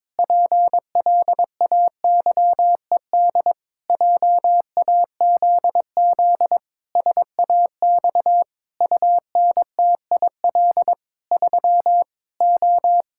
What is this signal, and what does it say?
Morse code, heard as a keyed tone, about 22 wpm, 700 Hz. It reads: PLAYED JAZZ SAX UNTIL 3 O
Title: Pangram Challenge v2 - 22wpm